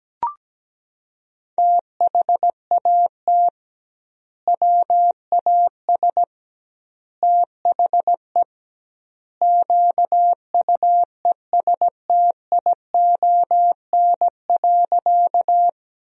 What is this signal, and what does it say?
Morse code, heard as a keyed tone, about 17 wpm, 700 Hz. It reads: THAT WAS THE QUESTION.
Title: 2x Word-Spacing - Sentences from Top 400 Words - 17wpm